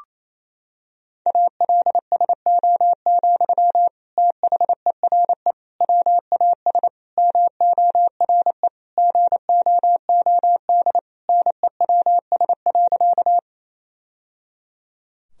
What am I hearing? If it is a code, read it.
Morse code, heard as a keyed tone, about 28 wpm, 700 Hz. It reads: ALSO, THERE WAS MORE GOOD NEWS.
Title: Sentences from Top 1000 Words - Review Part 1 - 28wpm